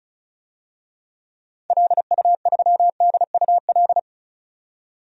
Morse code, heard as a keyed tone, about 34 wpm, 700 Hz. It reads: LU3DUL
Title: DXCC-100 Call Signs - 6 Characters - 35wpm